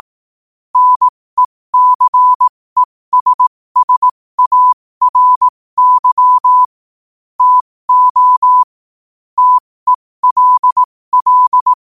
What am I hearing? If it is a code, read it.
Morse code, heard as a keyed tone, about 18 wpm, spaced as for 15 wpm, 1000 Hz. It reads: NECESSARY TO TELL
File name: qotdfiles / 15wpm